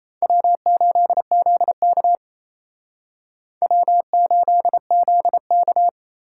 Morse code, heard as a keyed tone, about 37 wpm, 700 Hz. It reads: W8ZK W8ZK